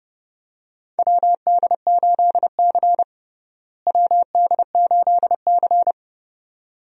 Morse code, heard as a keyed tone, about 32 wpm, 700 Hz. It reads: WD8C WD8C
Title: USA Call Signs - 4 Characters - Repeated 2x - 30wpm